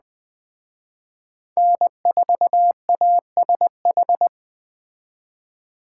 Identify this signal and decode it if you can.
Morse code, heard as a keyed tone, about 20 wpm, 700 Hz. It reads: N4ASH
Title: Speed Racing - USA 1x3 Call Signs 17wpm